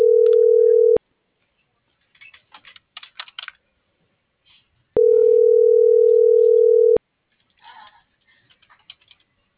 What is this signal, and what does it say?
Ringback tone (North America), 440 + 480 Hz.
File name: CENTRAL MABE / out xquenda.trinidad - .69452